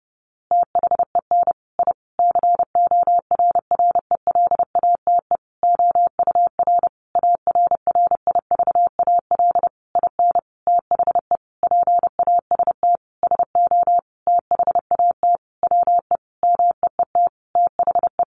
Morse code, heard as a keyed tone, about 30 wpm, 700 Hz. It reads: THEN I CORRELATE OUR ARRIVAL IN THE PAST SO THAT WE MEET THE
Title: A Sound Of Thunder 30wpm